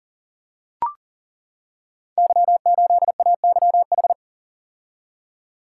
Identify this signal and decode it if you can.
Morse code, heard as a keyed tone, about 39 wpm, 700 Hz. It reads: Y8AYH
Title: Code Groups - 5 Letter:Numbers - 40wpm